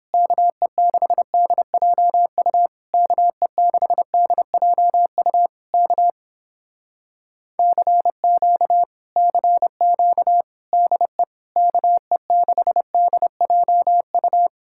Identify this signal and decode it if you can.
Morse code, heard as a keyed tone, about 29 wpm, 700 Hz. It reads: KE6DJU KE6DJU K CQ CQ DE KE6DJU